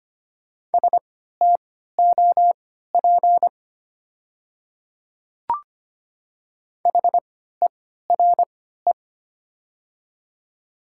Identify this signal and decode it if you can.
Morse code, heard as a keyed tone, about 25 wpm, spaced as for 14 wpm, 700 Hz. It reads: STOP HERE
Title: 3x Character Spacing - Top 300 Words - 25wpm